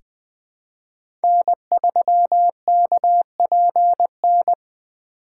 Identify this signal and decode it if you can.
Morse code, heard as a keyed tone, about 20 wpm, 700 Hz. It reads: N3KPN
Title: USA Call Signs - 2x Character Spacing - 20wpm